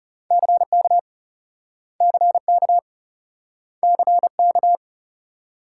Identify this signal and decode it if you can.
Morse code, heard as a keyed tone, about 38 wpm, 700 Hz. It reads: CK CK CK